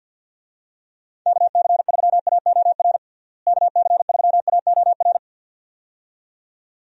Morse code, heard as a keyed tone, about 49 wpm, 700 Hz. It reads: KC3AOR KC3AOR
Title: USA Call Signs - 6 Characters - Repeated 2x - 50wpm